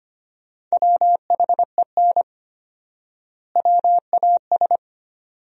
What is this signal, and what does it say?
Morse code, heard as a keyed tone, about 25 wpm, 700 Hz. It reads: WHEN WAS
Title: 4x Word-Spacing - Sentences from Top 100 Words - 25wpm